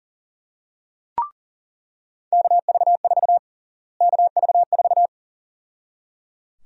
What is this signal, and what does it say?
Morse code, heard as a keyed tone, about 37 wpm, 700 Hz. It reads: KV4 KV4
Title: USA Call Sign Prefixes - Repeated 2x - 40wpm